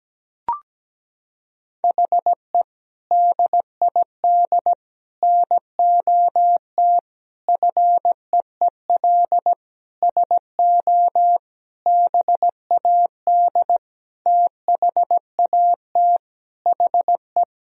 Morse code, heard as a keyed tone, about 17 wpm, 700 Hz. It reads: HE DID NOT FEEL SO BAD THAT HE
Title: Sentences from Top 1000 Words - Review Part 4 - 17wpm